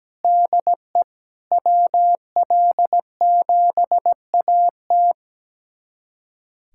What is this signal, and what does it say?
Morse code, heard as a keyed tone, about 17 wpm, 700 Hz. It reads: DE WL7AT
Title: USA Callsign DE Callsign 17wpm